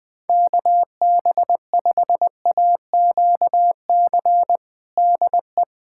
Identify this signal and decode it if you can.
Morse code, heard as a keyed tone, about 20 wpm, 700 Hz. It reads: KB5AQC DE
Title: USA Callsign DE Callsign - 6 Characters - 20wpm